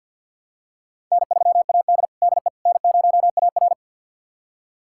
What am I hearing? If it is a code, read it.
Morse code, heard as a keyed tone, about 51 wpm, 700 Hz. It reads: N3AD DE N0AR
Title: USA Callsign DE Callsign - 4 Characters - 50wpm